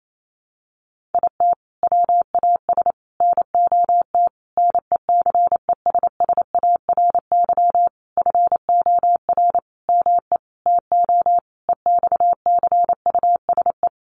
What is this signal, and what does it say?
Morse code, heard as a keyed tone, about 28 wpm, 700 Hz. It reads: IT WAS NOT NECESSARY FOR ME TO EXCUSE
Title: Sentences from Top 1300 Words 28wpm